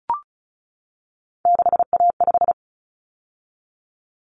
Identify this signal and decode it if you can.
Morse code, heard as a keyed tone, about 34 wpm, 700 Hz. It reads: 6A5
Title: Common Cut Numbers - 3-Digit 35wpm